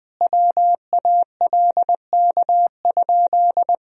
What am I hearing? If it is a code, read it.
Morse code, heard as a keyed tone, about 21 wpm, 700 Hz. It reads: WALK?
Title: Sentences from Top 1000 Words - Review Part 1 - 20wpm